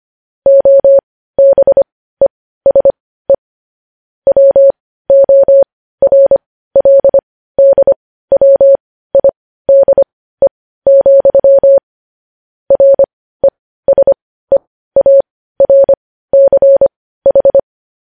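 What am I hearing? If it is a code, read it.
Morse code, heard as a keyed tone, about 25 wpm, spaced as for 15 wpm, 550 Hz. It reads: OBESE WORLDWIDE, RESEARCH
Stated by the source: Morse code transcription of news headlines.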